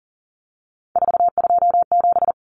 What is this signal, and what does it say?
Morse code, heard as a keyed tone, about 40 wpm, 700 Hz. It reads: 427
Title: Common Cut Numbers - 3-Digit 40wpm